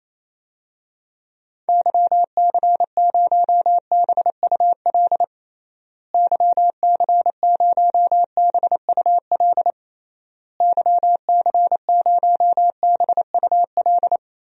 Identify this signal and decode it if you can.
Morse code, heard as a keyed tone, about 28 wpm, 700 Hz. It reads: YC0BUL YC0BUL YC0BUL
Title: DXCC-100 Call Signs - 6 Characters - Repeated 3x - 28wpm